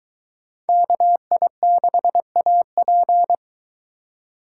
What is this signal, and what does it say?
Morse code, heard as a keyed tone, about 22 wpm, 700 Hz. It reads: KI6AP